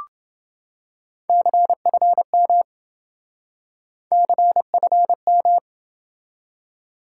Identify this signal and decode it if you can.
Morse code, heard as a keyed tone, about 26 wpm, 700 Hz. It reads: CFM CFM